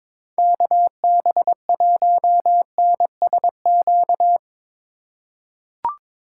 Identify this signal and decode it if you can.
Morse code, heard as a keyed tone, about 22 wpm, 700 Hz. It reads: KB1NSQ
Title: USA Call Signs - 6 Characters - 4x Character Spacing - 22wpm